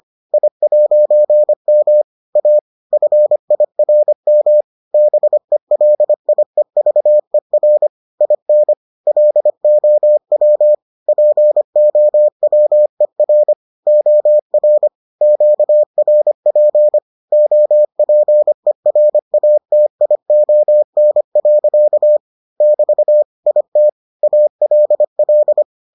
Morse code, heard as a keyed tone, about 25 wpm, 600 Hz. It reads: I'M A FIRM BELIEVER IN LOW POWER OR QRP OPERATION. <BT> IT ALL